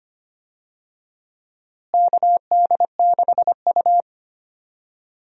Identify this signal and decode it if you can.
Morse code, heard as a keyed tone, about 25 wpm, 700 Hz. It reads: KD6U
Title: USA Call Signs - 4 Characters - Repeated 3x - 25wpm